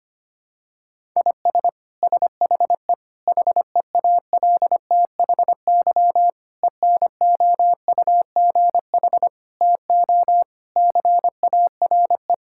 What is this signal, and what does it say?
Morse code, heard as a keyed tone, about 25 wpm, 700 Hz. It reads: IS SHE HEALTHY ENOUGH TO CARE
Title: Sentences from Top 1000 Words - Review Part 5 - 25wpm